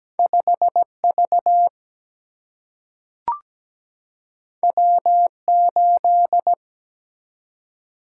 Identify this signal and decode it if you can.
Morse code, heard as a keyed tone, about 17 wpm, 700 Hz. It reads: 5V W8
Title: Single-Letter Single-Number 17wpm